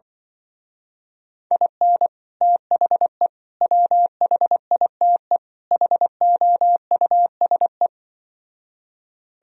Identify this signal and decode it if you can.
Morse code, heard as a keyed tone, about 24 wpm, 700 Hz. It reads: IN THE WHITE HOUSE
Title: Speed-Racing - Sets of 4 Words 20wpm